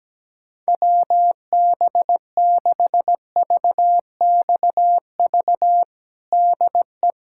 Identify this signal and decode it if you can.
Morse code, heard as a keyed tone, about 17 wpm, 700 Hz. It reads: WB6VXV DE
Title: USA Callsign DE Callsign 17wpm